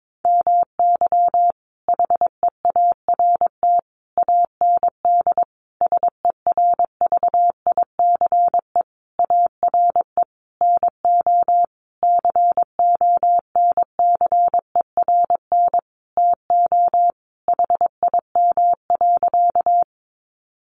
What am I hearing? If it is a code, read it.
Morse code, heard as a keyed tone, about 22 wpm, 700 Hz. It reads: MY HEART AND SERVICE ARE NO CONCERN TO HIM.
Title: Sentences from Top 1200 Words 22wpm